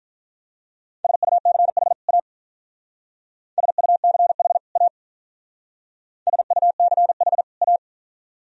Morse code, heard as a keyed tone, about 45 wpm, 700 Hz. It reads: SUCH A SUCH A SUCH A